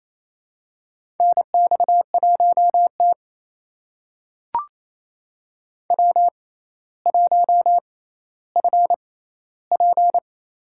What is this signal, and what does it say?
Morse code, heard as a keyed tone, about 28 wpm, spaced as for 14 wpm, 700 Hz. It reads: NX1T W1FP
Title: USA Call Signs - 4 Characters - 6x Character Spacing - 28wpm